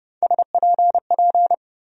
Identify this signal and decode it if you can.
Morse code, heard as a keyed tone, about 30 wpm, 700 Hz. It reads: SPP
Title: USA Call Sign Postfixes - 3 Letters 30wpm